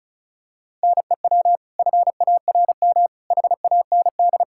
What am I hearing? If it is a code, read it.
Morse code, heard as a keyed tone, about 36 wpm, 700 Hz. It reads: NEW FARM HAND
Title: Pangram Challenge v2 - 35wpm